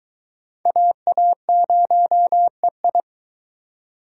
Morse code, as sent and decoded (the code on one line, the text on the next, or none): .- .- ----- . ..
AA0EI